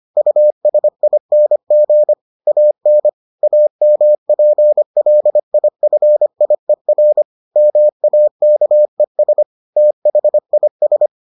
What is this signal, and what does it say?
Morse code, heard as a keyed tone, about 25 wpm, 600 Hz. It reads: USING AN AMPLIFIER MAKES THIS